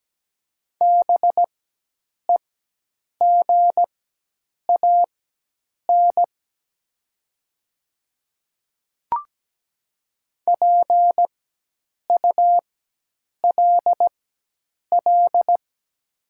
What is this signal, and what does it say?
Morse code, heard as a keyed tone, about 17 wpm, spaced as for 8 wpm, 700 Hz. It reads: BEGAN PULL
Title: 4x Character Spacing - Top 500 Words - 17wpm